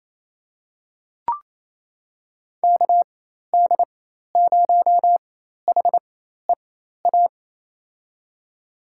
Morse code, heard as a keyed tone, about 27 wpm, spaced as for 13 wpm, 700 Hz. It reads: KD0HEA
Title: USA Call Signs - 6 Characters - 4x Character Spacing - 28wpm